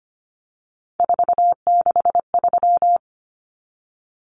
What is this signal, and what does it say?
Morse code, heard as a keyed tone, about 25 wpm, 700 Hz. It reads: 463
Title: Common Cut Numbers - 3-Digit 25wpm